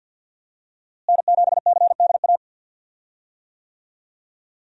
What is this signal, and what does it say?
Morse code, heard as a keyed tone, about 53 wpm, 700 Hz. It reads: N7CDA
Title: USA Call Signs - 5 Characters - Repeated 2x - 50wpm